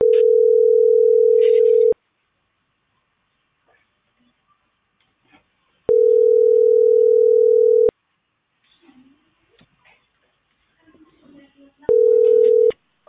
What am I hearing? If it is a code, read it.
Ringback tone (North America), 440 + 480 Hz.